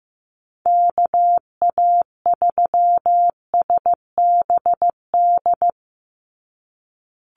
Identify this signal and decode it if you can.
Morse code, heard as a keyed tone, about 15 wpm, 700 Hz. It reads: KA3SBD
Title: USA Call Signs 15wpm